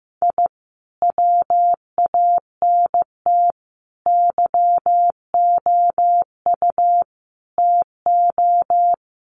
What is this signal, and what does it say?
Morse code, heard as a keyed tone, about 15 wpm, 700 Hz. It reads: I WANT YOU TO
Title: Sets of 5 Words Encore 15wpm